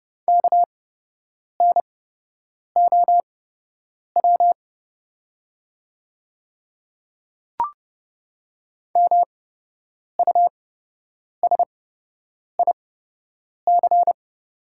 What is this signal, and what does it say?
Morse code, heard as a keyed tone, about 30 wpm, spaced as for 8 wpm, 700 Hz. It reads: KNOW MUSIC
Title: 8x Character Spacing - Top 500 Words - 30wpm